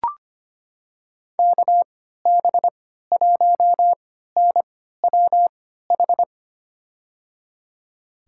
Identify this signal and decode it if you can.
Morse code, heard as a keyed tone, about 25 wpm, spaced as for 14 wpm, 700 Hz. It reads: KB1NWH
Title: USA Call Signs - 3x Character Spacing - 25wpm